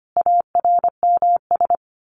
Morse code, heard as a keyed tone, about 25 wpm, 700 Hz. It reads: ARMS
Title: A Sound Of Thunder 25wpm